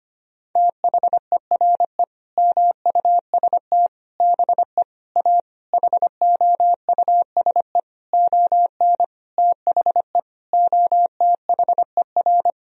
Morse code, heard as a keyed tone, about 25 wpm, 700 Hz. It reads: THERE MUST BE A HOUSE ON THE OTHER
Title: Sentences from Top 1000 Words - Review Part 1 - 25wpm